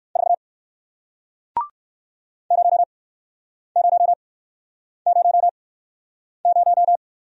Morse code, heard as a keyed tone, about 61 wpm, spaced as for 9 wpm, 700 Hz. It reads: 4 0000